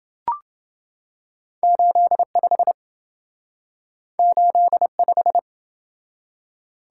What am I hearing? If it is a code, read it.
Morse code, heard as a keyed tone, about 28 wpm, 700 Hz. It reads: 85 85